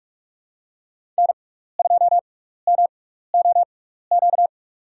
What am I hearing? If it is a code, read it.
Morse code, heard as a keyed tone, about 46 wpm, spaced as for 15 wpm, 700 Hz. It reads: N2MOQ
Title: USA Call Signs - 5 Characters - 6x Character Spacing - 45wpm